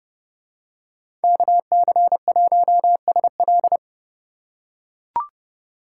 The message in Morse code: KC1SL

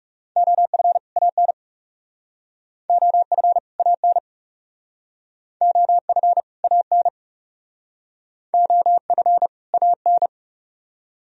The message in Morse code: OF AN OF AN OF AN OF AN

